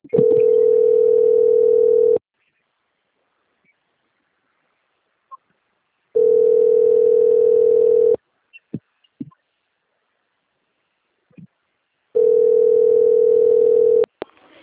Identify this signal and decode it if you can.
Ringback tone (North America), 440 + 480 Hz.